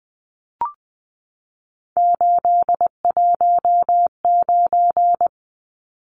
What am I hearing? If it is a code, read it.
Morse code, heard as a keyed tone, about 20 wpm, 700 Hz. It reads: 819
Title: Numbers - 3-Digit 20wpm